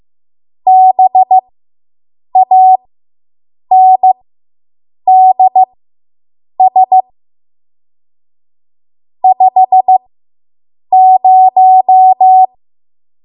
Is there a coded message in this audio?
Yes: Morse code from 0.7 to 12.5 s, heard as a keyed tone, about 15 wpm, spaced as for 7 wpm, 750 Hz. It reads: BANDS 50